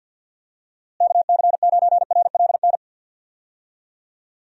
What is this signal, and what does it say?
Morse code, heard as a keyed tone, about 51 wpm, 700 Hz. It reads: KC9RLN
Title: USA Call Signs - 6 Characters - Repeated 2x - 50wpm